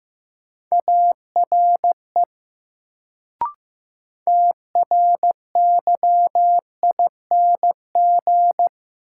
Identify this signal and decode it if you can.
Morse code, heard as a keyed tone, about 15 wpm, 700 Hz. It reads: ARE TRYING
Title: Sets of 2 Words - New Word Drill - 15wpm